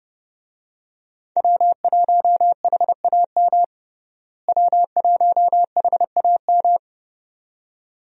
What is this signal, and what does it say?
Morse code, heard as a keyed tone, about 30 wpm, 700 Hz. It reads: W1HAM W1HAM
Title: USA Call Signs - 5 Characters - Repeated 2x - 30wpm